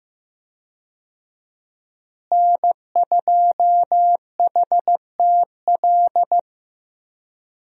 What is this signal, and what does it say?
Morse code, heard as a keyed tone, about 15 wpm, 700 Hz. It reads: N2HTL
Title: USA Call Signs - 5 Characters - Repeated 3x - 15wpm